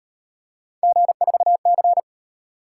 Morse code, heard as a keyed tone, about 35 wpm, 700 Hz. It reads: G4C